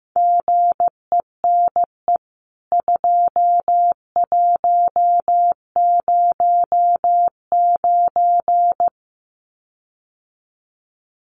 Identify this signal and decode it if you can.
Morse code, heard as a keyed tone, about 15 wpm, 700 Hz. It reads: GENE 2109
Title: CWT Exchange 15wpm